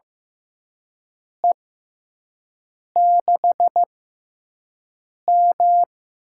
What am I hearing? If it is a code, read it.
Morse code, heard as a keyed tone, about 15 wpm, spaced as for 5 wpm, 700 Hz. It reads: E6M
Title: USA Call Signs - 6 Characters - 6x Character Spacing - 15wpm